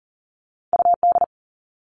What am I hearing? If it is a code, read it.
Morse code, heard as a keyed tone, about 42 wpm, 700 Hz. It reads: UD
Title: All Cut Numbers - 2-Digit 40wpm